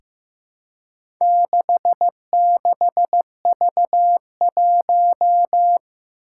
Morse code, heard as a keyed tone, about 15 wpm, 700 Hz. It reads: 66V1
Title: Four Challenging Letter-Number Combinations 15wpm